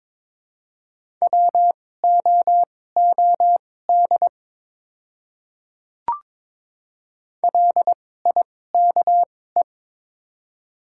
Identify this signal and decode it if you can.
Morse code, heard as a keyed tone, about 23 wpm, spaced as for 16 wpm, 700 Hz. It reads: WOOD LIKE
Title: 2x Character Spacing - Top 300 Words - 22wpm